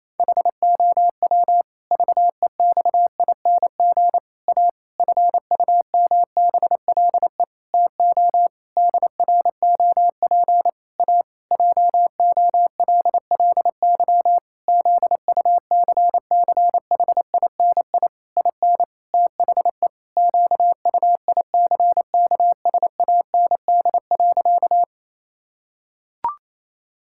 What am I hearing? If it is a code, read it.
Morse code, heard as a keyed tone, about 28 wpm, 700 Hz. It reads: HOW VEXING A FUMBLE TO DROP A JOLLY ZUCCHINI IN THE QUICKSAND.
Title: Pangram Challenge v2 - 28wpm